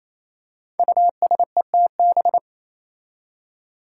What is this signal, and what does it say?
Morse code, heard as a keyed tone, about 27 wpm, 700 Hz. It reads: USETB
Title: Five Challenging Letter-Number Combinations 28wpm